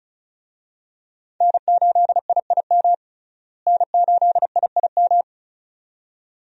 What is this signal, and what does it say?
Morse code, heard as a keyed tone, about 34 wpm, 700 Hz. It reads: N8IIM N8IIM
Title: USA Call Signs - 5 Characters - Repeated 2x - 35wpm